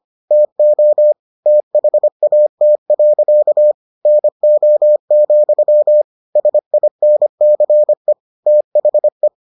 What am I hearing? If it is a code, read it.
Morse code, heard as a keyed tone, about 25 wpm, 600 Hz. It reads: TO THAT. NO, SINCE THE